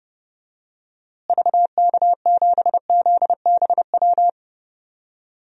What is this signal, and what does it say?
Morse code, heard as a keyed tone, about 32 wpm, 700 Hz. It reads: VK7ZBW
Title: DXCC-100 Call Signs - 6 Characters - Repeated 2x - 30wpm